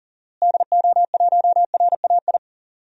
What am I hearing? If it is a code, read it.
Morse code, heard as a keyed tone, about 40 wpm, 700 Hz. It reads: DO1RAI
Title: DXCC-100 Call Signs - 6 Characters - Repeated 2x - 40wpm